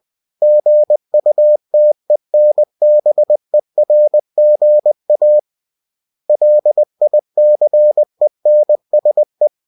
Morse code, heard as a keyed tone, about 20 wpm, 600 Hz. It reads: GUTENBERGA LICENSE
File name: cw-books / winnie-the-pooh / winnie-20wpm-page-0115